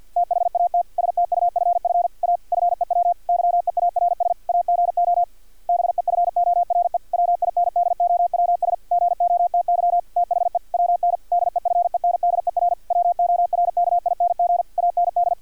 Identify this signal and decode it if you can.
Morse code, heard as a keyed tone, about 50 wpm, 700 Hz, with radio noise. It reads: THAT STUFF A FEW YEARS AGO BEFORE WINDOWS GOTY THE WN DEFENDER WORKING AND
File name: archived / CRSnet